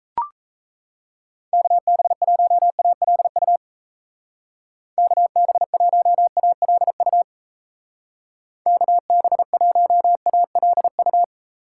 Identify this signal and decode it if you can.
Morse code, heard as a keyed tone, about 34 wpm, 700 Hz. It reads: KB1ALU KB1ALU KB1ALU
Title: Speed Racing - USA 2x3 Call Signs 28wpm